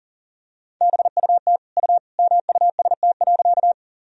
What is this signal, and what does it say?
Morse code, heard as a keyed tone, about 40 wpm, 700 Hz. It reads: BUT U MUST.
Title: Sentences from Top 200 Words and Abbreviations 40wpm